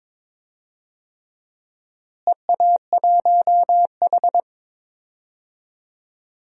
Morse code, heard as a keyed tone, about 22 wpm, 700 Hz. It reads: EA1H
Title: DXCC-100 Call Signs - 4 Characters - Repeated 3x - 22wpm